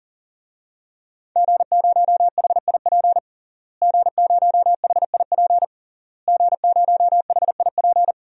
Morse code, heard as a keyed tone, about 40 wpm, 700 Hz. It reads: G0HIP G0HIP G0HIP
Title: DXCC-100 Call Signs - 5 Characters - Repeated 3x - 40wpm